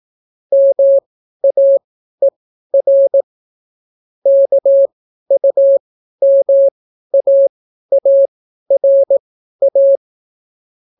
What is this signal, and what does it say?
Morse code, heard as a keyed tone, about 18 wpm, spaced as for 12 wpm, 550 Hz. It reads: MAER KUMAARA